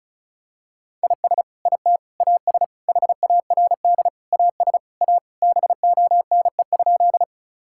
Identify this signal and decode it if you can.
Morse code, heard as a keyed tone, about 34 wpm, 700 Hz. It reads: IS IT AS HARD AS A BONE?
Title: Sentences from Top 1600 Words 35wpm